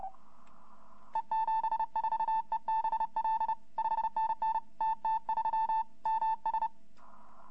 Morse code, heard as a keyed tone, about 30 wpm, 890 Hz. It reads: E74EBL 5NN TT3 MS
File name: hfkup2018 / greske / yu2v